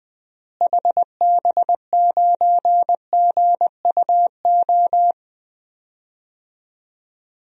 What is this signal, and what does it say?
Morse code, heard as a keyed tone, about 20 wpm, 700 Hz. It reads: HB9GUO
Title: DXCC-100 Call Signs - 6 Characters - Repeated 3x - 20wpm